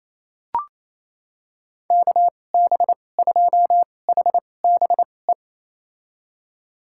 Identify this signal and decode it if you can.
Morse code, heard as a keyed tone, about 28 wpm, spaced as for 20 wpm, 700 Hz. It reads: KB2HBE
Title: USA Call Signs - 6 Characters - 2x Character Spacing - 28wpm